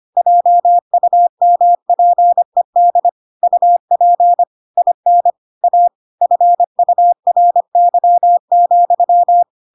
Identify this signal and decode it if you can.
Morse code, heard as a keyed tone, about 25 wpm, 700 Hz. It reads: JUMPED UP IN A FURY,
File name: Pinocchio / Pinocchio ch4 25 25 ik1hge